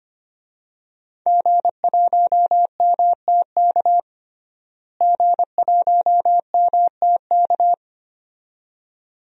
Morse code, heard as a keyed tone, about 25 wpm, 700 Hz. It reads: G1MTK G1MTK
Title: DXCC-100 Call Signs - 5 Characters - Repeated 2x - 25wpm